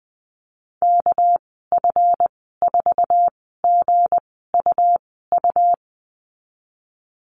Morse code, heard as a keyed tone, about 20 wpm, spaced as for 14 wpm, 700 Hz. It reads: KF4GUU
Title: 213 Lesson 34g - Number 8 - Call Signs 20wpm